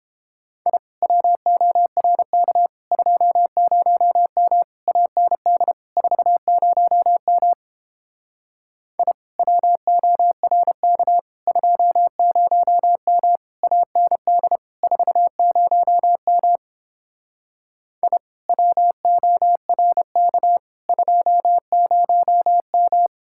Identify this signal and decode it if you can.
Morse code, heard as a keyed tone, about 29 wpm, 700 Hz. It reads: I WORK 20M AND 40M I WORK 20M AND 40M I WORK 20M